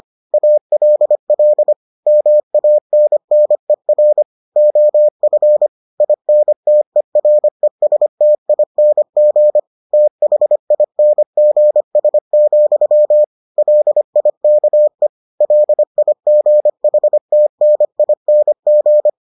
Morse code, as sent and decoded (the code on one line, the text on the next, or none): .- .-.. .-.. / -- .- -. -. .-. / --- ..-. / .. -. - . .-. ... - .. -. --. / - .... .. -. --. ... --..-- / .-.. .. -.- . / .-.. .. --. .... - -. .. -. --.
ALL MANNER OF INTERESTING THINGS, LIKE LIGHTNING